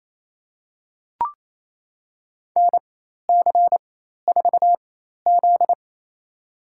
Morse code, heard as a keyed tone, about 28 wpm, spaced as for 13 wpm, 700 Hz. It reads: NC4Z
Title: USA Call Signs - 4 Characters - 4x Character Spacing - 28wpm